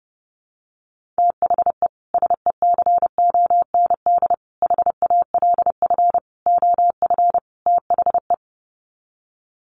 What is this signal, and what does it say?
Morse code, heard as a keyed tone, about 30 wpm, 700 Hz. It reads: THE SECOND HALF OF THE
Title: Sets of 5 Words 30wpm